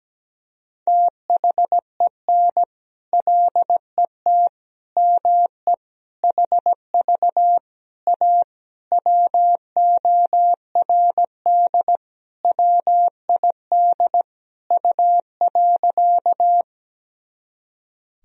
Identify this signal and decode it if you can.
Morse code, heard as a keyed tone, about 17 wpm, 700 Hz. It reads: THEN LET ME HV A WORD WID U.
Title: Sentences from Top 300 Words and Abbreviations 17wpm